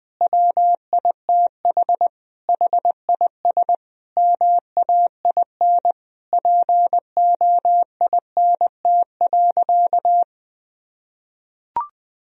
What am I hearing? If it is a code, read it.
Morse code, heard as a keyed tone, about 20 wpm, 700 Hz. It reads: WITH HIS MAIN POINT.
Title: Sentences from Top 1000 Words - Review Part 2 - 20wpm